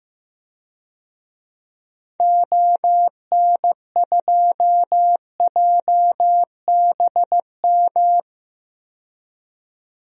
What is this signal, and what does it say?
Morse code, heard as a keyed tone, about 16 wpm, 700 Hz. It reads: ON2JBM
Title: DXCC-100 Call Signs - 6 Characters - 15wpm